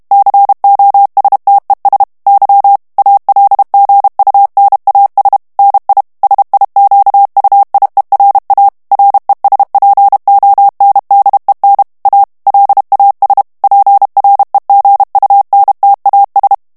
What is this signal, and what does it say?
Morse code, heard as a keyed tone, about 32 wpm, 800 Hz. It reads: COSTES Y ALGUNAS NI SIQUIERA RESPONDEN A LAS PREGUNTAS
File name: morse mp3 / ESPAGNOL TEXTE EN CLAIR 1 VITESSE 30 MOTS MINUTES